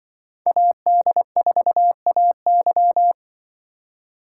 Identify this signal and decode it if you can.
Morse code, heard as a keyed tone, about 24 wpm, 700 Hz. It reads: AD4AY
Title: Speed Racing - USA 2x2 Call Signs 20wpm